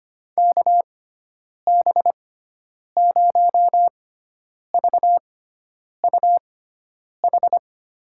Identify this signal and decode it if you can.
Morse code, heard as a keyed tone, about 25 wpm, spaced as for 9 wpm, 700 Hz. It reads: KB0VUH
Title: USA Call Signs - 6 Characters - 6x Character Spacing - 25wpm